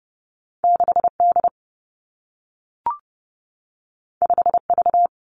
Morse code, heard as a keyed tone, about 30 wpm, 700 Hz. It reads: 6 D 5 V
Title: Two Challenging Letter-Number Combinations 30wpm